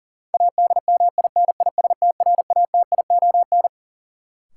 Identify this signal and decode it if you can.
Morse code, heard as a keyed tone, about 37 wpm, 700 Hz. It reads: ADMINISTRATION
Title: Sentences from Top 1300 Words - New Word Drill Alt - 40wpm